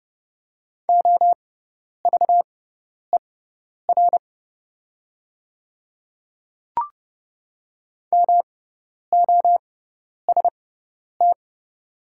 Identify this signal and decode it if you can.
Morse code, heard as a keyed tone, about 30 wpm, spaced as for 10 wpm, 700 Hz. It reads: OVER MOST
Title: 6x Character Spacing - Top 100 Words - 30wpm